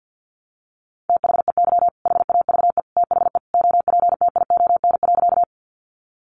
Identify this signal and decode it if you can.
Morse code, heard as a keyed tone, about 50 wpm, 700 Hz. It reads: THEY HAVE THE OPTION.
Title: Sentences from Top 1300 Words 50wpm